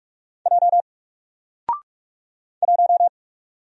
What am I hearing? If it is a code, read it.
Morse code, heard as a keyed tone, about 45 wpm, 700 Hz. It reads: J 1
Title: Single Letter-Number Alt 45wpm